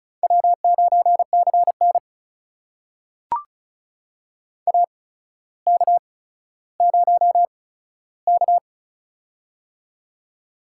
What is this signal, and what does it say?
Morse code, heard as a keyed tone, about 36 wpm, spaced as for 15 wpm, 700 Hz. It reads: W9CN AK0K
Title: USA Call Signs - 4 Characters - 8x Character Spacing - 35wpm